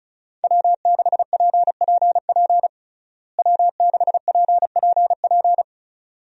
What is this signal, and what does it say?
Morse code, heard as a keyed tone, about 36 wpm, 700 Hz. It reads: W6PPP W6PPP
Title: USA Call Signs - 5 Characters - Repeated 2x - 35wpm